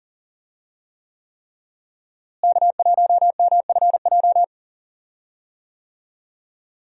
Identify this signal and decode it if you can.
Morse code, heard as a keyed tone, about 40 wpm, 700 Hz. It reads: K1MFJ
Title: USA Call Signs - 5 Characters - Repeated 3x - 40wpm